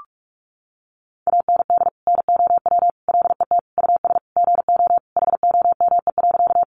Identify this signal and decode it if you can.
Morse code, heard as a keyed tone, about 44 wpm, 700 Hz. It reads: AND NOW LET US GO HOME.
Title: Sentences from Top 300 Words 45wpm